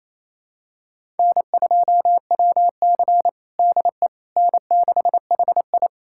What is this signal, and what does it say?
Morse code, heard as a keyed tone, about 28 wpm, 700 Hz. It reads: N2WC DE N6HI
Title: USA Callsign DE Callsign - 4 Characters - 28wpm